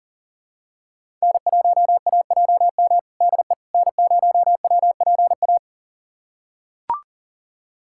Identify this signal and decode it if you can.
Morse code, heard as a keyed tone, about 40 wpm, 700 Hz. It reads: N1AJM DE N0WPA
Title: USA Callsign DE Callsign 40wpm